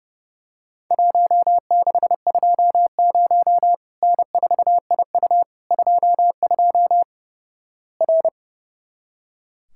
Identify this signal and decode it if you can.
Morse code, heard as a keyed tone, about 28 wpm, 700 Hz. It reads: 1620 N4IU 22 R
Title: Worked All Europe DX Contest - QTC Exchange - 30wpm